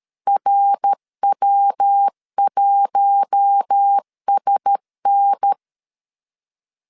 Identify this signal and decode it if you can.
Morse code, heard as a keyed tone, about 13 wpm, 800 Hz. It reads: RW1SN
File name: LEZIONI / 6 PKX0 / 6 PKX0 12wpm 2231